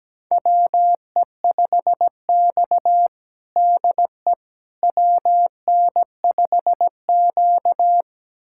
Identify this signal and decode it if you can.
Morse code, heard as a keyed tone, about 17 wpm, 700 Hz. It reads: WE5X DE WN5Q
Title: USA Callsign DE Callsign - 4 Characters - 17wpm